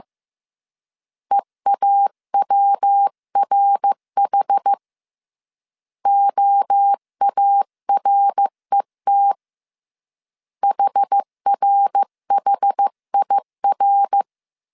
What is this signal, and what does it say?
Morse code, heard as a keyed tone, about 15 wpm, 800 Hz. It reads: EAWRH OARET HRHIR
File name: CW / DET / 1 3 1500C S A 14wpm 2447